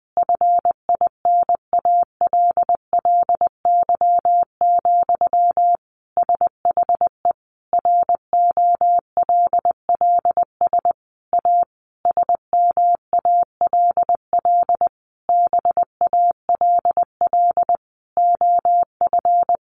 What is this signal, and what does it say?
Morse code, heard as a keyed tone, about 19 wpm, 700 Hz. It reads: FINALLY, SHE ROLLS A SMALL BALL OF
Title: Easy Phrases 20wpm